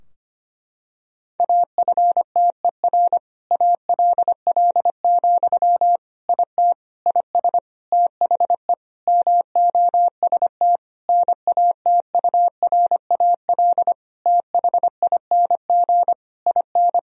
Morse code, heard as a keyed tone, about 25 wpm, 700 Hz. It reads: AFTER ALL, IT IS THE MOST NATURAL THING IN
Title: Sentences from Top 1000 Words - Review Part 4 - 25wpm